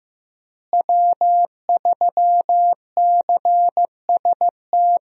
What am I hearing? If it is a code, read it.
Morse code, heard as a keyed tone, about 15 wpm, 700 Hz. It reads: W3CST
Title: USA Call Signs - 5 Characters - Repeated 2x - 15wpm